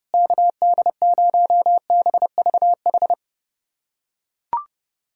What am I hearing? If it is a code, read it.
Morse code, heard as a keyed tone, about 30 wpm, 700 Hz. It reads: KD0BVH
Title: USA Callsign DE Callsign - 6 Characters - 30wpm